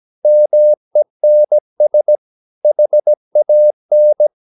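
Morse code, heard as a keyed tone, about 17 wpm, 600 Hz. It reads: MENS HAN
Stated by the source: Slow (QRS) Morse practice at 17 wpm, from CW med Gnister.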